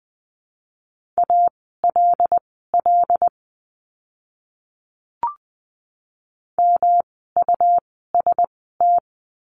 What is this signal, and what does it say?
Morse code, heard as a keyed tone, about 19 wpm, spaced as for 14 wpm, 700 Hz. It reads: ALL MUST
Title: 237 Lesson 38f - Letter q - Words 20wpm